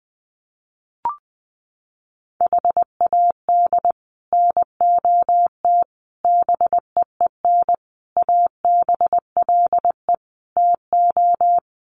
Morse code, heard as a keyed tone, about 20 wpm, 700 Hz. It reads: HAD NOT BEEN ABLE TO
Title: Sets of 5 Words Encore 20wpm